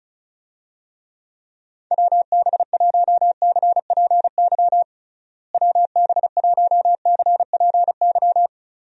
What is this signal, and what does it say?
Morse code, heard as a keyed tone, about 34 wpm, 700 Hz. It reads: WB1CPY WB1CPY
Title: USA Call Signs - 6 Characters - Repeated 2x - 35wpm